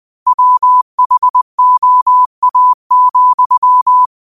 Morse code, heard as a keyed tone, about 20 wpm, 1000 Hz. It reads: WHOA,
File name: qotdfiles / 20wpm